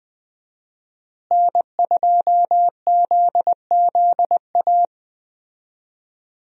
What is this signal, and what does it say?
Morse code, heard as a keyed tone, about 20 wpm, 700 Hz. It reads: N2ZZA
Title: USA Call Signs - 5 Characters - Repeated 2x - 20wpm